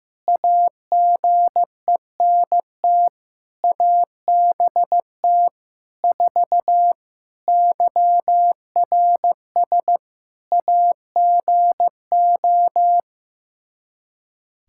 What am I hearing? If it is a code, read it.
Morse code, heard as a keyed tone, about 16 wpm, 700 Hz. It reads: AGENT ABT 4 YRS AGO
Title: K1USN SST QSO Practice 15wpm